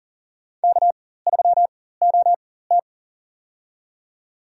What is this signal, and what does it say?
Morse code, heard as a keyed tone, about 40 wpm, spaced as for 19 wpm, 700 Hz. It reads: K3OT
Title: USA Call Signs - 4 Characters - 4x Character Spacing - 40wpm